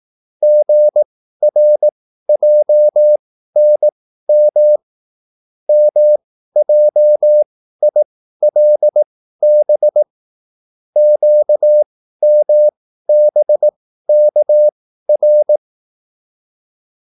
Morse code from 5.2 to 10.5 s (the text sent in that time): MJILB